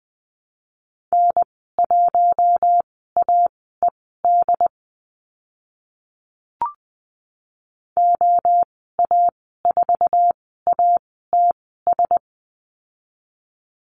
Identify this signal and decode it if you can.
Morse code, heard as a keyed tone, about 20 wpm, spaced as for 14 wpm, 700 Hz. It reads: N1AED OA4ATS
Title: 076 Lesson 13g - Letter L - Call Signs 20wpm